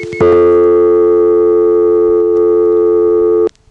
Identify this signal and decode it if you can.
Dial tone (North America), 350 + 440 Hz.